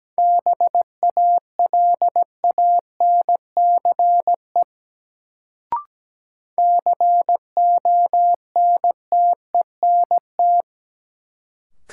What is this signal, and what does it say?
Morse code, heard as a keyed tone, about 17 wpm, 700 Hz. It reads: BALANCE CONTENT
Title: Sentences from Top 900 Words - New Word Drill - 17wpm